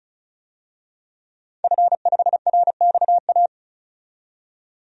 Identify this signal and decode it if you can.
Morse code, heard as a keyed tone, about 36 wpm, 700 Hz. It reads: F5RXA
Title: DXCC-100 Call Signs - 5 Characters - 35wpm